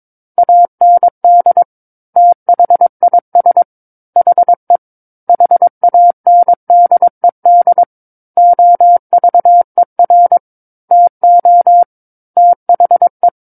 Morse code, heard as a keyed tone, about 22 wpm, 700 Hz. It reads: AND THIS HE HANDED OVER TO THE
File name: 06-Chapter-06 22 WPM